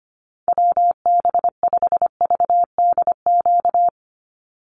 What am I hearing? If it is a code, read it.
Morse code, heard as a keyed tone, about 25 wpm, 700 Hz. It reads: WB5VDQ
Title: USA Call Signs 25wpm